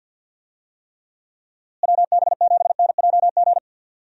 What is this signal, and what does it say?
Morse code, heard as a keyed tone, about 47 wpm, 700 Hz. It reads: WB7NJG
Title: USA Call Signs - 6 Characters - Repeated 3x - 50wpm